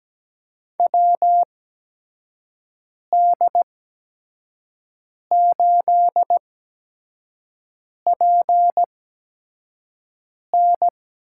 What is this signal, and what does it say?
Morse code, heard as a keyed tone, about 17 wpm, spaced as for 5 wpm, 700 Hz. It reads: WD8PN
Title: USA Call Signs - 6 Characters - 8x Character Spacing - 17wpm